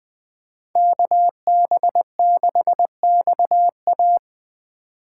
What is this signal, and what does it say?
Morse code, heard as a keyed tone, about 20 wpm, 700 Hz. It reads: KB6XA